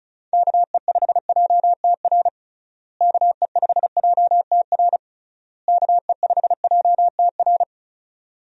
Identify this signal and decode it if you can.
Morse code, heard as a keyed tone, about 34 wpm, 700 Hz. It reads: KE5JTR KE5JTR KE5JTR
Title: USA Call Signs - 6 Characters - Repeated 3x - 35wpm